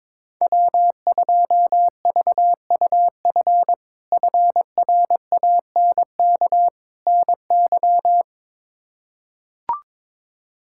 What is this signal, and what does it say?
Morse code, heard as a keyed tone, about 22 wpm, 700 Hz. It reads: W2VUF FRANK NY
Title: K1USN SST QSO Practice - 22wpm